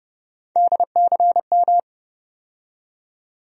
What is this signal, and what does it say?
Morse code, heard as a keyed tone, about 31 wpm, 700 Hz. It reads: DCM
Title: USA Call Sign Postfixes - 3 Letters 30wpm